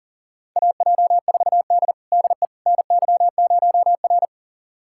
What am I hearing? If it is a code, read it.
Morse code, heard as a keyed tone, about 40 wpm, 700 Hz. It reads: AJ4D DE NY0R
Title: USA Callsign DE Callsign - 4 Characters - 40wpm